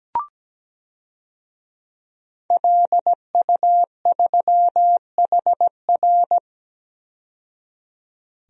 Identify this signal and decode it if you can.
Morse code, heard as a keyed tone, about 17 wpm, 700 Hz. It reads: LU3HR
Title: DXCC-100 Call Signs - 5 Characters - 17wpm